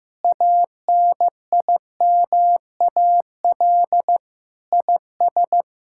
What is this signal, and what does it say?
Morse code, heard as a keyed tone, about 15 wpm, 700 Hz. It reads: ANIMAL IS
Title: Sentences from Top 1000 Words - Review Part 5 - 15wpm